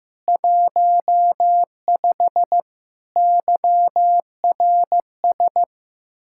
Morse code, heard as a keyed tone, about 15 wpm, 700 Hz. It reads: 15 YRS
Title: K1USN SST QSO Practice 15wpm